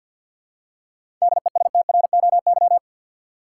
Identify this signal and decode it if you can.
Morse code, heard as a keyed tone, about 52 wpm, 700 Hz. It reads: DESTROY
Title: Seven Letter Words - 50wpm